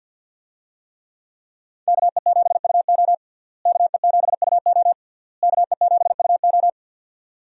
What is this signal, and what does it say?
Morse code, heard as a keyed tone, about 47 wpm, 700 Hz. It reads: KE7UO KE7UO KE7UO
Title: USA Call Signs - 5 Characters - Repeated 3x - 50wpm